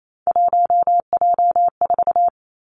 Morse code, heard as a keyed tone, about 28 wpm, 700 Hz. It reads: 1J4
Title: Three Challenging Letter-Number Combinations 28wpm